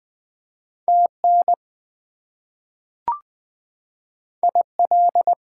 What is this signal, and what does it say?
Morse code, heard as a keyed tone, about 20 wpm, 700 Hz. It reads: TN IL
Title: US States and Canadian Provinces 20wpm